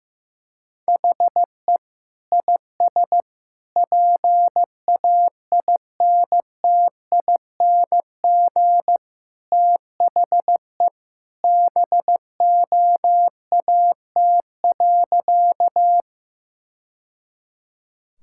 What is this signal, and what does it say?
Morse code, heard as a keyed tone, about 15 wpm, 700 Hz. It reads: HE IS PAINTING THE BOAT.
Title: Sentences from Top 1000 Words - Review Part 6 - 15wpm